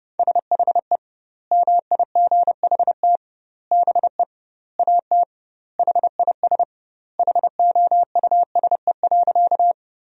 Morse code, heard as a keyed tone, about 30 wpm, 700 Hz. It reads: SHE MIGHT BE AT HIS HOUSE.
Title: 2x Word-Spacing - Sentences from Top 300 Words - 30wpm